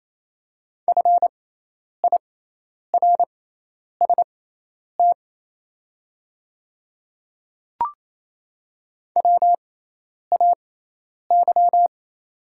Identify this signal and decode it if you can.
Morse code, heard as a keyed tone, about 28 wpm, spaced as for 10 wpm, 690 Hz. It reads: FIRST WAY
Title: 6x Character Spacing - Top 200 Words - 28wpm